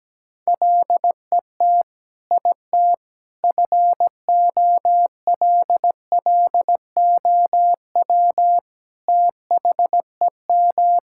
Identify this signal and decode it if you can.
Morse code, heard as a keyed tone, about 17 wpm, 700 Hz. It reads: LET IT FOLLOW THEM
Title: Sentences from Top 1600 Words 17wpm